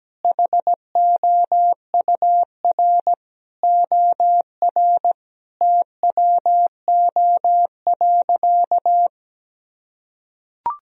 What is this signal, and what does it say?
Morse code, heard as a keyed tone, about 17 wpm, 700 Hz. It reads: HOUR OR TWO.
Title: Sentences from Top 1000 Words - Review Part 4 - 17wpm